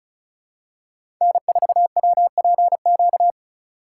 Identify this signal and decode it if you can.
Morse code, heard as a keyed tone, about 36 wpm, 700 Hz. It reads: N4WPQ
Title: USA Call Signs - 4x Character Spacing - 35wpm